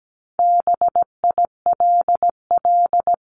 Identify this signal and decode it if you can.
Morse code, heard as a keyed tone, about 17 wpm, 700 Hz. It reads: BILL
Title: CWT Exchange 17wpm